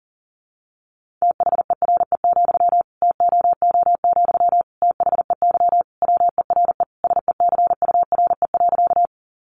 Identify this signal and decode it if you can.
Morse code, heard as a keyed tone, about 40 wpm, 700 Hz. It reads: THERE, TOO, THEY WERE SECURE.
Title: Sentences from Top 1200 Words 40wpm